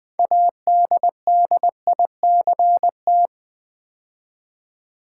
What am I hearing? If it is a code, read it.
Morse code, heard as a keyed tone, about 20 wpm, 700 Hz. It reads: ADDICT
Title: Build A Word - 3 Letters At A Time - 9-Letter Words - 20wpm